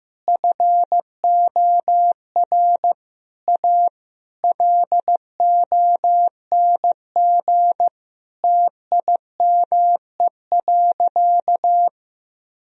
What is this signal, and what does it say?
Morse code, heard as a keyed tone, about 15 wpm, 700 Hz. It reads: FOR A LONG TIME.
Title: Sentences from Top 1000 Words - Review Part 4 - 15wpm